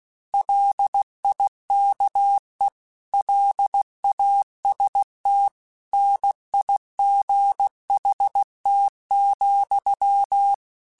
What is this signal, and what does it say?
Morse code, heard as a keyed tone, about 16 wpm, 800 Hz. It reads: LIKE LAST NIGHT,
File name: Weather Sets / Weather Sets-mp3s / Weather 4 17 @15